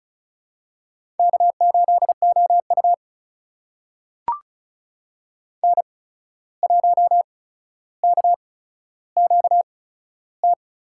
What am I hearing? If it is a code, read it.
Morse code, heard as a keyed tone, about 34 wpm, spaced as for 14 wpm, 700 Hz. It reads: K8OU N1KQT
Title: USA Call Signs - 8x Character Spacing - 35wpm